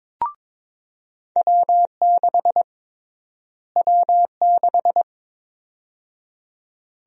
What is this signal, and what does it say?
Morse code, heard as a keyed tone, about 22 wpm, 700 Hz. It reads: W6 W6
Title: USA Call Sign Prefixes - Repeated 2x - 22wpm